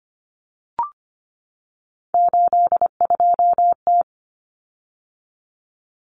Morse code, heard as a keyed tone, about 25 wpm, 700 Hz. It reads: 82T
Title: Common Cut Numbers - 3-Digit 25wpm